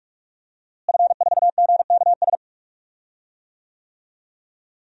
Morse code, heard as a keyed tone, about 45 wpm, 700 Hz. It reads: F4GKS
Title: DXCC-100 Call Signs - 5 Characters - Repeated 2x - 45wpm